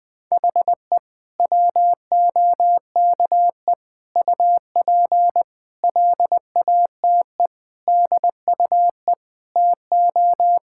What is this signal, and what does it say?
Morse code, heard as a keyed tone, about 20 wpm, 700 Hz. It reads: HE WOKE UP LATE DUE TO
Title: Sentences with 4-Word Phrases - Spoken Beginning - 20wpm